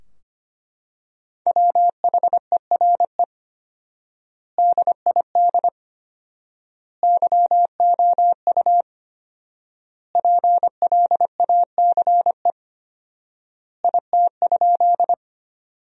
Morse code, heard as a keyed tone, about 25 wpm, 700 Hz. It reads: WHERE DID YOU PLACE IT?
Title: 4x Word-Spacing - Sentences from Top 200 Words - 25wpm